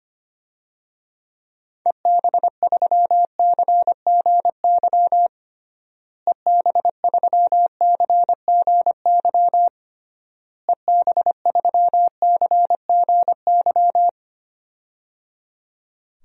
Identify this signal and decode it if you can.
Morse code, heard as a keyed tone, about 25 wpm, 700 Hz. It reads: EB3CGY EB3CGY EB3CGY
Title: DXCC-100 Call Signs - 6 Characters - Repeated 3x - 25wpm